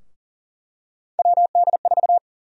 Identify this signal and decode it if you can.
Morse code, heard as a keyed tone, about 38 wpm, 700 Hz. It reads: WD4
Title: USA Call Sign Prefixes - 40wpm